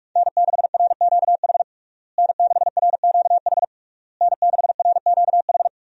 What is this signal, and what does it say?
Morse code, heard as a keyed tone, about 42 wpm, 700 Hz. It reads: N6RQH N6RQH N6RQH
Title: USA Call Signs - 5 Characters - Repeated 3x - 45wpm